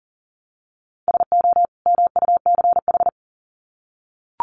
Morse code, heard as a keyed tone, about 39 wpm, 700 Hz. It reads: SO MUCH
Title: Sets of 2 Words 40wpm